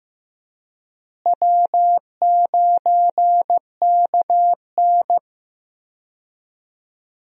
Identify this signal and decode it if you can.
Morse code, heard as a keyed tone, about 15 wpm, 700 Hz. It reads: W9KN
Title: USA Call Signs - 4 Characters - Repeated 2x - 15wpm